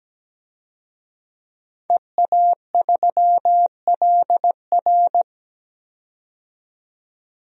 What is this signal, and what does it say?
Morse code, heard as a keyed tone, about 17 wpm, 700 Hz. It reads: EA3LR
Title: DXCC-100 Call Signs - 5 Characters - 17wpm